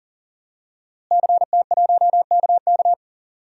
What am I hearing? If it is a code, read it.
Morse code, heard as a keyed tone, about 40 wpm, 700 Hz. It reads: CT1KK
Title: DXCC-100 Call Signs - 5 Characters - Repeated 3x - 40wpm